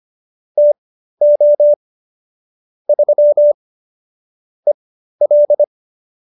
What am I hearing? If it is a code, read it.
Morse code, heard as a keyed tone, about 25 wpm, spaced as for 13 wpm, 590 Hz. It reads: TO 3 EL